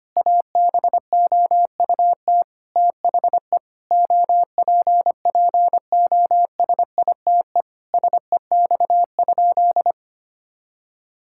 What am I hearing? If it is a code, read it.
Morse code, heard as a keyed tone, about 25 wpm, 700 Hz. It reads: ABOUT THE OPPOSITE SEX?
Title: Sentences from Top 1600 Words 25wpm